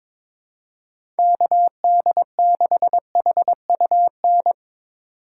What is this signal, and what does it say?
Morse code, heard as a keyed tone, about 22 wpm, 700 Hz. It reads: KD6HUN
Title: USA Call Signs - 6 Characters - 4x Character Spacing - 22wpm